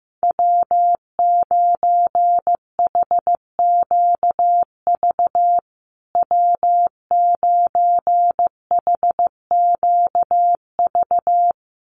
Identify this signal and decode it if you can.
Morse code, heard as a keyed tone, about 15 wpm, 700 Hz. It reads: W9HQV W9HQV
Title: USA Call Signs Repeated 2x 15wpm